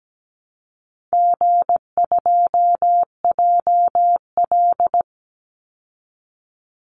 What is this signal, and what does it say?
Morse code, heard as a keyed tone, about 17 wpm, 700 Hz. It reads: G2JL
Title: Easy Call Signs Repeated 3x - 17wpm